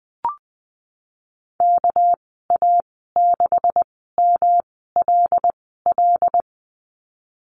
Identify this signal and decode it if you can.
Morse code, heard as a keyed tone, about 20 wpm, spaced as for 14 wpm, 700 Hz. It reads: KA6MLL
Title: 200 Lesson 32g - Letter k - Call Signs 20wpm